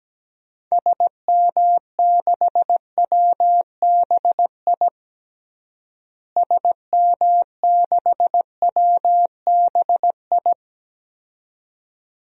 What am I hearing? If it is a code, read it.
Morse code, heard as a keyed tone, about 17 wpm, 700 Hz. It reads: SM6WBI SM6WBI
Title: DXCC-100 Call Signs - 6 Characters - Repeated 2x - 17wpm